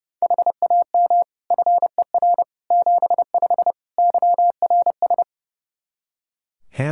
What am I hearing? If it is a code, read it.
Morse code, heard as a keyed tone, about 30 wpm, 700 Hz. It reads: HAM FER 75 YRS
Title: K1USN SST QSO Practice 30wpm